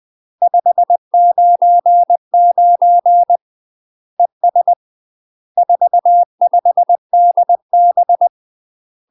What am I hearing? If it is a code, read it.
Morse code, heard as a keyed tone, about 20 wpm, 700 Hz. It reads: 599 ES 45DB
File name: UK MASTER QSOs TRAINING/20WPM